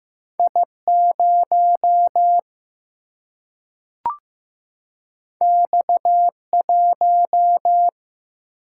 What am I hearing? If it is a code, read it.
Morse code, heard as a keyed tone, about 15 wpm, 700 Hz. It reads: I 0 X 1
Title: Single-Letter Single-Number 15wpm